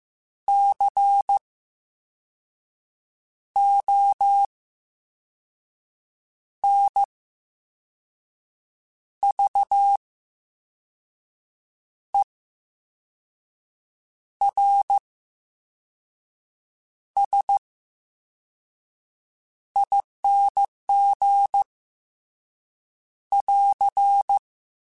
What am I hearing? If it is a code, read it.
Morse code, heard as a keyed tone, about 15 wpm, spaced as for 4 wpm, 800 Hz. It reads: CONVERSING <AR>
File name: Special INGs-mp3s / Special INGs-mp3s / VerySlowToFaster Ing @13